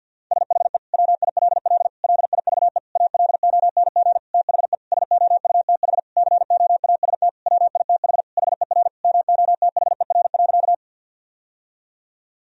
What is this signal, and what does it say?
Morse code, heard as a keyed tone, about 50 wpm, 700 Hz. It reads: SHE WILL LIVE ALONG THE SOUTH COAST WITH HER MOTHER.
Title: Sentences from Top 1000 Words - Review Part 4 - 50wpm